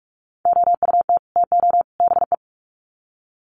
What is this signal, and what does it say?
Morse code, heard as a keyed tone, about 44 wpm, 700 Hz. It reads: OUT TO BE
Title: Sets of 3 Words 45wpm